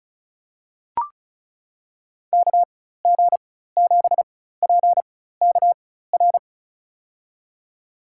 Morse code, heard as a keyed tone, about 36 wpm, spaced as for 16 wpm, 700 Hz. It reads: KG7PKR
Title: USA Call Signs - 6 Characters - 4x Character Spacing - 35wpm